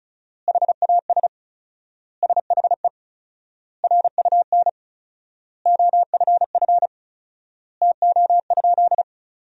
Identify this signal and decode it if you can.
Morse code, heard as a keyed tone, about 34 wpm, 700 Hz. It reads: HAS SHE RUN OFF TO?
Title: 4x Word-Spacing - Sentences from Top 300 Words - 35wpm